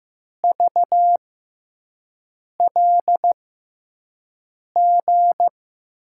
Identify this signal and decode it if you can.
Morse code, heard as a keyed tone, about 15 wpm, spaced as for 5 wpm, 700 Hz. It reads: VLG
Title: USA Call Signs - 5 Characters - 6x Character Spacing - 15wpm